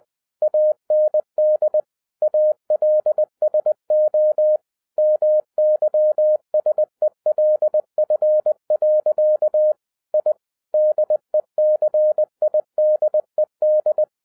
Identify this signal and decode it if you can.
Morse code, heard as a keyed tone, about 20 wpm, 610 Hz. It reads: AND ALSO MYSELF. I DECIDED
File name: 2011-07-mp3 / n58 BACK FROM A BIG ACCIDENT 20wpm